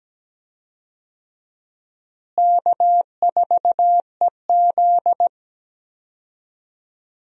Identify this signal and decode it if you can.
Morse code, heard as a keyed tone, about 17 wpm, 700 Hz. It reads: K4EZ
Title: USA Call Signs - 4 Characters - Repeated 3x - 17wpm